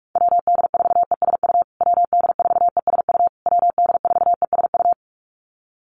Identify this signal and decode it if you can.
Morse code, heard as a keyed tone, about 44 wpm, 690 Hz. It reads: WD4ESU WD4ESU WD4ESU
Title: USA Call Signs Repeated 3x 45wpm